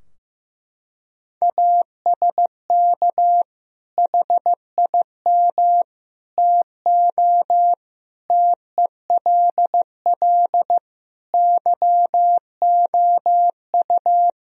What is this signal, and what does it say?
Morse code, heard as a keyed tone, about 15 wpm, 700 Hz. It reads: ASK HIM TO TELL YOU
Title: Sentences from Top 1000 Words - Review Part 3 - 15wpm